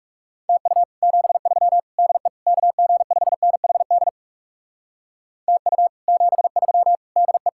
Morse code, heard as a keyed tone, about 37 wpm, 700 Hz. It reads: TU 73 DE KG5NHD TU 73 DE
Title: Speed-Racing Common QSO Phrases Encore - 30wpm